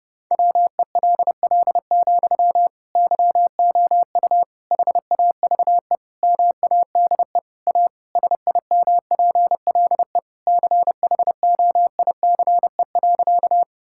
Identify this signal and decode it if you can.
Morse code, heard as a keyed tone, about 32 wpm, 700 Hz. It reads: WELL, YOU HAVE MADE A SIMPLE CHOICE.
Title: Sentences from Top 1000 Words - Review Part 6 - 30wpm